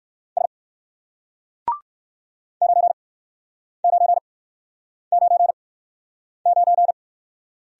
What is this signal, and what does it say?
Morse code, heard as a keyed tone, about 61 wpm, spaced as for 9 wpm, 700 Hz. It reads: S 9999